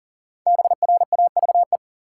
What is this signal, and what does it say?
Morse code, heard as a keyed tone, about 40 wpm, 700 Hz. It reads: BRAVE
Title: Five Letter Words - 40wpm